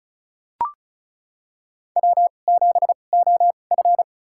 Morse code, heard as a keyed tone, about 34 wpm, spaced as for 25 wpm, 700 Hz. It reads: W7OF
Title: USA Call Signs - 2x Character Spacing - 35wpm